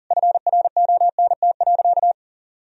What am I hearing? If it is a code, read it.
Morse code, heard as a keyed tone, about 40 wpm, 700 Hz. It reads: FRONT.
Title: Sentences from Top 1000 Words - Review Part 2 - 40wpm